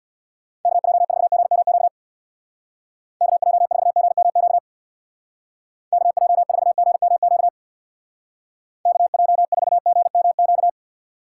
Morse code, heard as a keyed tone, about 60 wpm, 700 Hz. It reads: KJ4GMQ KJ4GMQ KJ4GMQ KJ4GMQ
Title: Speed Racing - USA 2x3 Call Signs 50wpm